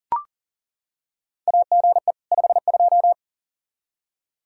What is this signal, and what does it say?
Morse code, heard as a keyed tone, about 38 wpm, 700 Hz. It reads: AGE 52
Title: Practice QSO - Exchange 3 - More Info - 40wpm